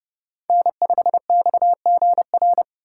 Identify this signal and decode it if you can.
Morse code, heard as a keyed tone, about 30 wpm, 700 Hz. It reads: N5XGR
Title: USA Call Signs - 5 Characters - 4x Character Spacing - 30wpm